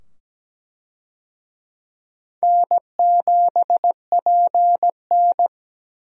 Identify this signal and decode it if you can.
Morse code, heard as a keyed tone, about 17 wpm, 700 Hz. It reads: N7PN
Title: USA Call Signs - 4 Characters - 17wpm